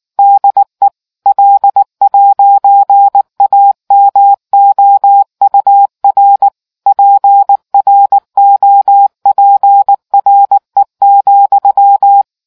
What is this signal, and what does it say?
Morse code, heard as a keyed tone, about 19 wpm, 800 Hz. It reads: DE L'AMOUR PROPRE,